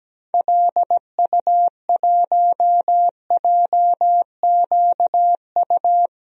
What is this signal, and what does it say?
Morse code, heard as a keyed tone, about 17 wpm, 700 Hz. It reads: LU1JQU
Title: DXCC-100 Call Signs - 6 Characters - Repeated 3x - 17wpm